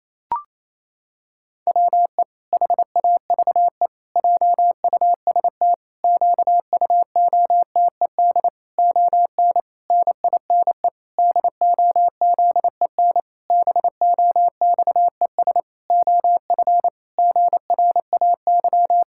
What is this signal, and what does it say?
Morse code, heard as a keyed tone, about 27 wpm, 700 Hz. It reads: WE HAVE JUST QUOTED ON NINE DOZEN BOXES OF GRAY
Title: Pangram Challenge v2 - 28wpm